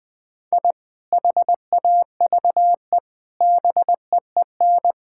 Morse code, heard as a keyed tone, about 21 wpm, 700 Hz. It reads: I HAVE BEEN
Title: Sentences with 3-Word Phrases - 20wpm